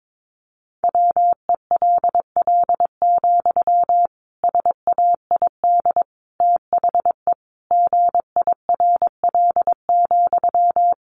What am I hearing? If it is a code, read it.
Morse code, heard as a keyed tone, about 22 wpm, 700 Hz. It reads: WELL, SAID THE GIRL,
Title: Sentences from Top 400 Words 22wpm